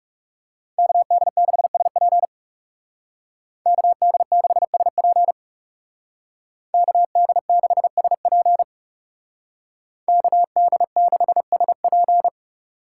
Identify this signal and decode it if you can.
Morse code, heard as a keyed tone, about 34 wpm, 700 Hz. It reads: KD6SP KD6SP KD6SP KD6SP
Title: Speed Racing - USA 2x2 Call Signs 30wpm